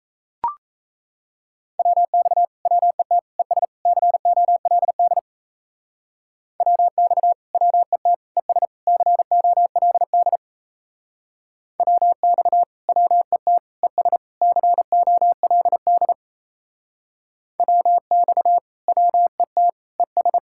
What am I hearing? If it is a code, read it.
Morse code, heard as a keyed tone, about 36 wpm, 700 Hz. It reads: WX WET ES COLD WX WET ES COLD WX WET ES COLD WX WET ES